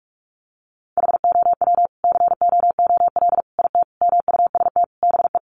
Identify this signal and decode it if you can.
Morse code, heard as a keyed tone, about 44 wpm, 700 Hz. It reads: HOW COOL IT MUST BE
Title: Sentences from Top 1200 Words 45wpm